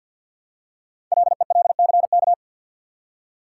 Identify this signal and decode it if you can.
Morse code, heard as a keyed tone, about 48 wpm, 700 Hz. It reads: LELCK
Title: Code Groups - 5 Letters - 50wpm